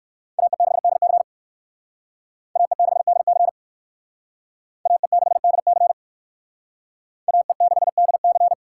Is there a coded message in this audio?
Yes: Morse code from 0.4 to 8.5 s, heard as a keyed tone, about 56 wpm, 700 Hz. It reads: AE6DC AE6DC AE6DC AE6DC